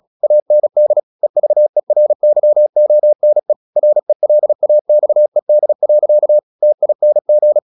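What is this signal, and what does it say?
Morse code, heard as a keyed tone, about 36 wpm, 600 Hz. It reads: AND EVERYONE RELAXED. TING
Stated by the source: Morse practice at 36 wpm, text of Ken Follett's novel Never, from CW med Gnister.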